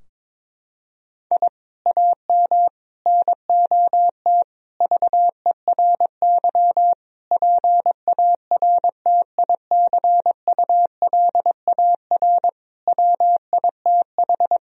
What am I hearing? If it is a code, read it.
Morse code, heard as a keyed tone, about 22 wpm, 700 Hz. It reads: I AM NOT VERY PARTICULAR WITH
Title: Sentences from Top 1000 Words - Review Part 4 - 22wpm